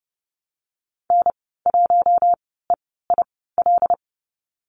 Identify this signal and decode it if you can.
Morse code, heard as a keyed tone, about 30 wpm, spaced as for 17 wpm, 700 Hz. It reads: N1EIL
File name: Lesson 14g - Number 2 - Call Signs 30wpm